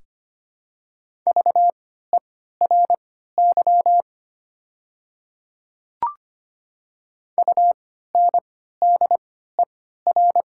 Morse code, heard as a keyed tone, about 25 wpm, spaced as for 14 wpm, 700 Hz. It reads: VERY UNDER
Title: 3x Character Spacing - Top 200 Words - 25wpm